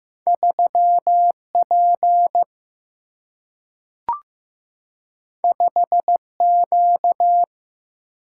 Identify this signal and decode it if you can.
Morse code, heard as a keyed tone, about 15 wpm, 700 Hz. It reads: 3 P 5 Q
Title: Single-Letter Single-Number 15wpm